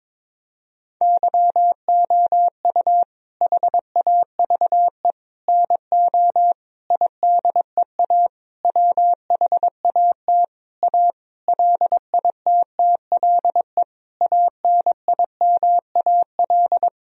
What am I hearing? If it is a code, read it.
Morse code, heard as a keyed tone, about 22 wpm, 700 Hz. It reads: YOU HAVE NO IDEA WHAT A LITTLE ANIMAL
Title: Sentences from Top 1000 Words - Review Part 1 - 22wpm